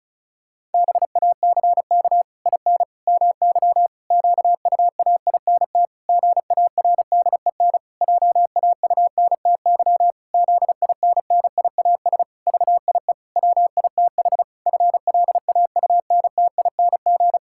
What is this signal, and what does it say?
Morse code, heard as a keyed tone, about 34 wpm, 700 Hz. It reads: BACK IN MY QUAINT GARDEN JAUNTY ZINNIAS VIE WITH FLAUNTING
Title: Pangram Challenge v2 - 35wpm